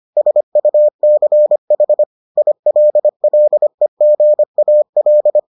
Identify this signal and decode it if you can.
Morse code, heard as a keyed tone, about 25 wpm, 600 Hz. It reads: SUCH ILLEGAL